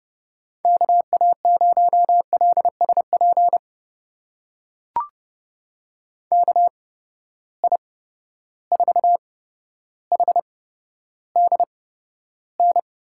Morse code, heard as a keyed tone, about 30 wpm, spaced as for 13 wpm, 700 Hz. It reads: KA0LSP KI4HDN
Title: USA Call Signs - 6 Characters - 8x Character Spacing - 30wpm